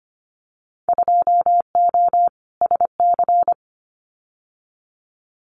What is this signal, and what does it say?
Morse code, heard as a keyed tone, about 25 wpm, 700 Hz. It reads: 2O SC
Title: Winter Field Day Exchanges 25wpm